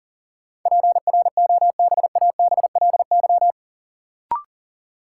Morse code, heard as a keyed tone, about 40 wpm, 700 Hz. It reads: PROBABLY
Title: Sentences from Top 700 Words - New Word Drill - 40wpm